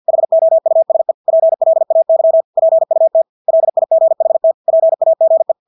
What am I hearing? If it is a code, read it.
Morse code, heard as a keyed tone, about 50 wpm, 650 Hz. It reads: HOUSE PLAY PUT LIGHT PAGE
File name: Engelske ord 50 wpm